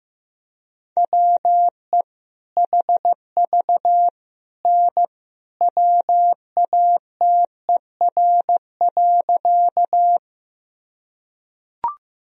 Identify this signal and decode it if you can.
Morse code, heard as a keyed tone, about 15 wpm, 700 Hz. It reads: WE HV N WATER.
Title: Sentences from Top 100 Words and Abbreviations 15wpm